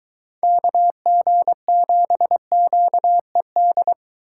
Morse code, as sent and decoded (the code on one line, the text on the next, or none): -.- --. --... --.- . -..
KG7QED